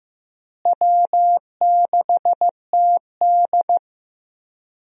Morse code, heard as a keyed tone, about 15 wpm, 700 Hz. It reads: W6TD
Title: USA Call Signs - 4 Characters - Repeated 2x - 15wpm